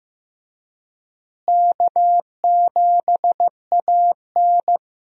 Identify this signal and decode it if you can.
Morse code, heard as a keyed tone, about 15 wpm, 700 Hz. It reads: K7AN
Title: USA Call Signs - 4 Characters - Repeated 2x - 15wpm